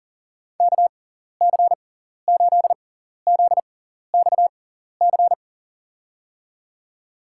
Morse code, heard as a keyed tone, about 40 wpm, spaced as for 14 wpm, 700 Hz. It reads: KC8ZXC
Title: USA Call Signs - 6 Characters - 6x Character Spacing - 40wpm